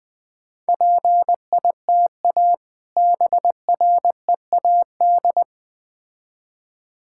Morse code, heard as a keyed tone, about 20 wpm, 700 Hz. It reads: PITA BREAD
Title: Difficult Words with Hints - 20wpm